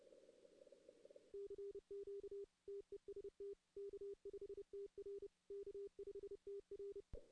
Morse code, heard as a keyed tone, about 30 wpm, 400 Hz. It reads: CQ TEST K5TR K5TR
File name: 2016 arrl dx cw / 2016-ARRL-DX-CW-K5TR- - 0-3